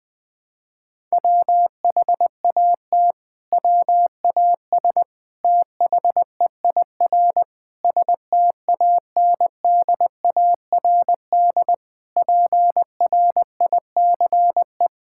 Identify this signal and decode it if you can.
Morse code, heard as a keyed tone, about 21 wpm, 700 Hz. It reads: WHAT WAS THEIR STANDARD PRICE
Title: Sentences from Top 1000 Words - Review Part 6 - 20wpm